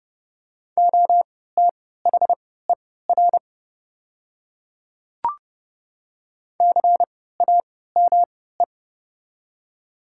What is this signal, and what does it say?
Morse code, heard as a keyed tone, about 29 wpm, spaced as for 17 wpm, 700 Hz. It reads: OTHER CAME
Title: 3x Character Spacing - Top 200 Words - 30wpm